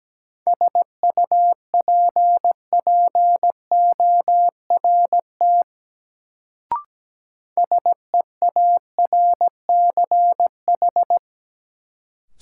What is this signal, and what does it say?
Morse code, heard as a keyed tone, about 17 wpm, 700 Hz. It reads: SUPPORT SEARCH
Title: Sentences from Top 800 Words - New Word Drill - 17wpm